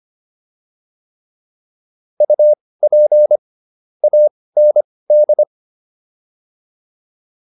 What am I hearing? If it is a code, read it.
Morse code, heard as a keyed tone, about 25 wpm, spaced as for 18 wpm, 600 Hz. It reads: UP AND